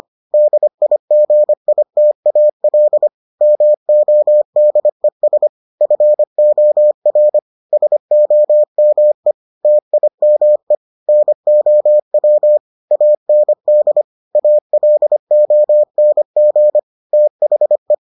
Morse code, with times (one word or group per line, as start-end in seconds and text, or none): 0.3-3.1 DIGITAL
3.4-5.5 MODES
5.8-7.4 FOR
7.7-9.3 SOME
9.6-10.8 TIME
11.1-12.6 NOW
12.9-14.0 AND
14.4-16.8 ALONG
17.1-17.9 THE